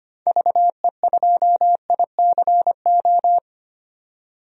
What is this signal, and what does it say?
Morse code, heard as a keyed tone, about 25 wpm, 700 Hz. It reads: VE2ICO
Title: DXCC-100 Call Signs - 6 Characters - 25wpm